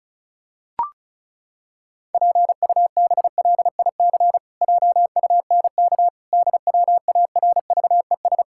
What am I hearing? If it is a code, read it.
Morse code, heard as a keyed tone, about 34 wpm, 700 Hz. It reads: PUBLIC JUNK DWARVES
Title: Pangram Challenge v2 - 35wpm